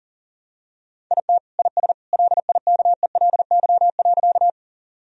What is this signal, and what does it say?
Morse code, heard as a keyed tone, about 37 wpm, 700 Hz. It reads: IT IS LIKELY.
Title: Sentences from Top 1000 Words - Review Part 4 - 40wpm